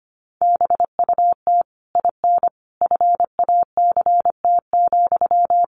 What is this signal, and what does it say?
Morse code, heard as a keyed tone, about 25 wpm, 700 Hz. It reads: BUT IN FACT,
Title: Sentences from Top 1500 Words 25wpm